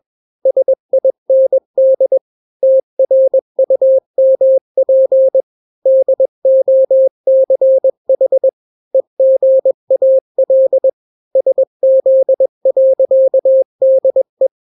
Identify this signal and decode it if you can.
Morse code, heard as a keyed tone, about 21 wpm, 530 Hz. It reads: SIND TRUMP DOCH EGAL SZ.DE
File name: daily / 20wpm